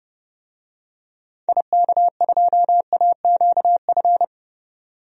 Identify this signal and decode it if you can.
Morse code, heard as a keyed tone, about 30 wpm, 700 Hz. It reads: IK2AQF
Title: DXCC-100 Call Signs - 6 Characters - 30wpm